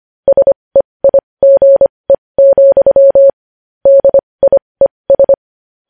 Morse code, heard as a keyed tone, about 25 wpm, spaced as for 20 wpm, 550 Hz. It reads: SEIGE, DIES
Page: news-headlines-in-morse-code-at-20-wpm